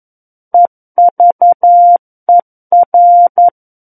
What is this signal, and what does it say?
Morse code, heard as a keyed tone, about 11 wpm, 700 Hz. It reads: EVER
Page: SKCC - War of the Worlds - 02-Chapter 2 - 11 WPM